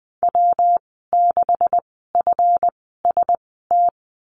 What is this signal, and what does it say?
Morse code, heard as a keyed tone, about 20 wpm, spaced as for 14 wpm, 700 Hz. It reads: W6FST
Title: 147 Lesson 24g - Letter y - Call Signs 20wpm